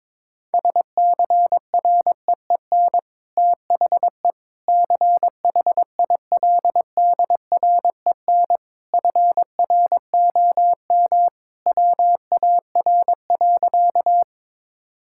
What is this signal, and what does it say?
Morse code, heard as a keyed tone, about 22 wpm, 700 Hz. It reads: SCREEN THE CHILDREN FROM WAR.
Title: Sentences from Top 1000 Words - Review Part 6 - 22wpm